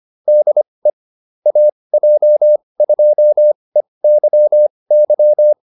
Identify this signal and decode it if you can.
Morse code, heard as a keyed tone, about 25 wpm, spaced as for 20 wpm, 600 Hz. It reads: DE AJ2EYY